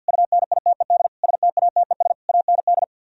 Morse code, heard as a keyed tone, about 61 wpm, 700 Hz. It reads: UNITED STATES AND
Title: Speed-Racing - Sets of 5 Words 50wpm